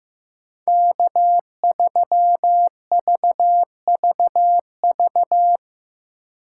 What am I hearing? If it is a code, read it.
Morse code, heard as a keyed tone, about 15 wpm, 700 Hz. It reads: K3VVV
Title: USA Call Signs - 2x Character Spacing - 15wpm